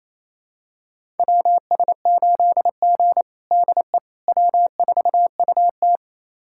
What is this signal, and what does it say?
Morse code, heard as a keyed tone, about 28 wpm, 700 Hz. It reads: WS8G DE W4UT
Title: USA Callsign DE Callsign - 4 Characters - 28wpm